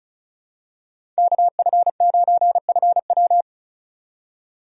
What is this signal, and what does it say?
Morse code, heard as a keyed tone, about 36 wpm, 700 Hz. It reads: KF9FW
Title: USA Call Signs - 5 Characters - 35wpm